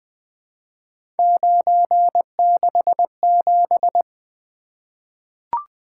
Morse code, heard as a keyed tone, about 20 wpm, 700 Hz. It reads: 967
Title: Three Letter-Number Combinations - 20wpm